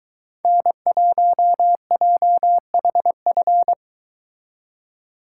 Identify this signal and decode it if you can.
Morse code, heard as a keyed tone, about 23 wpm, 700 Hz. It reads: N1JHF